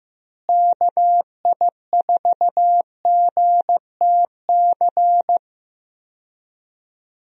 Morse code, heard as a keyed tone, about 15 wpm, 700 Hz. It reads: KI4GTC
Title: USA Call Signs - 6 Characters - Repeated 2x - 15wpm